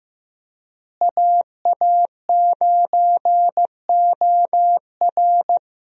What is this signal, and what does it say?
Morse code, heard as a keyed tone, about 15 wpm, 700 Hz. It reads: AA9OR
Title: USA Call Signs - 5 Characters - Repeated 3x - 15wpm